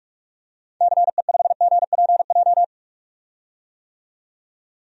Morse code, heard as a keyed tone, about 41 wpm, 690 Hz. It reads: CE5GPJ
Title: DXCC-100 Call Signs - 6 Characters - 45wpm